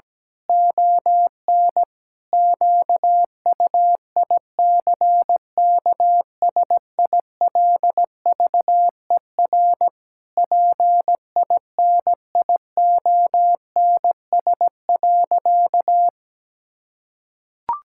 Morse code, heard as a keyed tone, about 17 wpm, 700 Hz. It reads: ON QUICKSILVER PINIONS.
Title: Pangram Challenge v2 - 17wpm